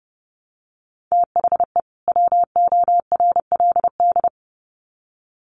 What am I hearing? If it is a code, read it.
Morse code, heard as a keyed tone, about 30 wpm, 700 Hz. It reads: THE WORLD
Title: Sets of 2 Words 30wpm